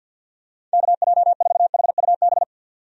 Morse code, heard as a keyed tone, about 50 wpm, 700 Hz. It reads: KJ4HUB
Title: USA Call Signs - 6 Characters - 50wpm